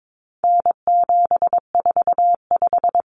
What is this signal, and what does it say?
Morse code, heard as a keyed tone, about 22 wpm, 700 Hz. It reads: N745
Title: Common Cut Numbers - 4-Digit 22wpm